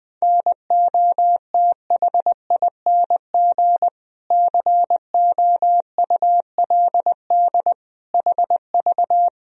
Morse code, heard as a keyed tone, about 20 wpm, 700 Hz. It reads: NOTHING COULD HV
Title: Sentences from Top 500 Words and Abbreviations 20wpm